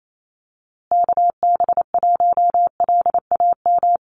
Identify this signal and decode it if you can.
Morse code, heard as a keyed tone, about 28 wpm, 700 Hz. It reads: KB1LAM
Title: USA Call Signs Repeated 3x 28wpm